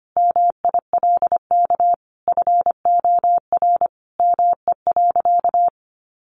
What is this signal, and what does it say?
Morse code, heard as a keyed tone, about 25 wpm, 700 Hz. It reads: MILK FOR ME.
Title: Sentences from Top 1500 Words 25wpm